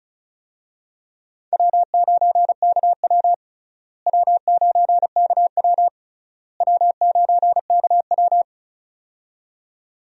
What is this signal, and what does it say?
Morse code, heard as a keyed tone, about 34 wpm, 700 Hz. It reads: W9KW W9KW W9KW
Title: USA Call Signs - 4 Characters - Repeated 3x - 35wpm